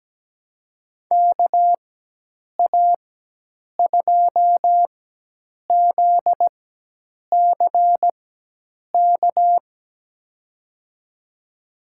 Morse code, heard as a keyed tone, about 17 wpm, spaced as for 8 wpm, 700 Hz. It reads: KA2ZCK
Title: USA Call Signs - 6 Characters - 4x Character Spacing - 17wpm